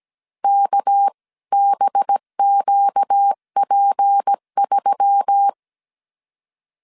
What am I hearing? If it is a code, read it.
Morse code, heard as a keyed tone, about 17 wpm, 800 Hz. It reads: KBQP3
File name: 16wpm